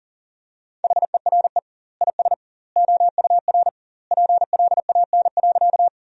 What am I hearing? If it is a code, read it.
Morse code, heard as a keyed tone, about 40 wpm, 700 Hz. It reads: HERE IS OUR PLAN.
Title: 2x Word-Spacing - Sentences from Top 500 Words - 40wpm